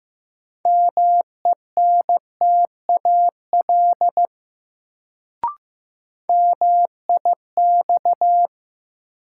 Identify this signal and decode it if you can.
Morse code, heard as a keyed tone, about 15 wpm, 700 Hz. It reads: MENTAL MIX
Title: Sentences from Top 1000 Words - Review Part 4 - New Word Drill - 15wpm